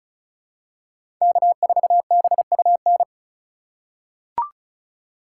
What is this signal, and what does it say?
Morse code, heard as a keyed tone, about 36 wpm, 700 Hz. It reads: K4BUN
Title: USA Call Signs - 3x Character Spacing - 35wpm